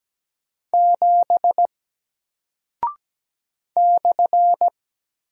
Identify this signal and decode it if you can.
Morse code, heard as a keyed tone, about 17 wpm, 700 Hz. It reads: 7 /
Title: Single Character (Letter:Number:Punctuation:Prosign) 17wpm